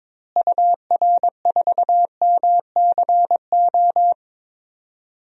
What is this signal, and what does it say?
Morse code, heard as a keyed tone, about 22 wpm, 700 Hz. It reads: UR4MCO
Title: DXCC-100 Call Signs - 6 Characters - Repeated 2x - 22wpm